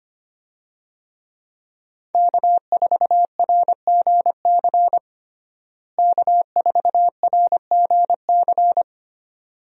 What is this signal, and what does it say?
Morse code, heard as a keyed tone, about 25 wpm, 700 Hz. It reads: K4RGC K4RGC
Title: USA Call Signs - 5 Characters - Repeated 2x - 25wpm